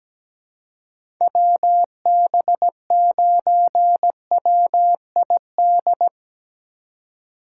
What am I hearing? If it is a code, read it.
Morse code, heard as a keyed tone, about 17 wpm, 700 Hz. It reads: WB9WID
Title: USA Call Signs - 6 Characters - 2x Character Spacing - 17wpm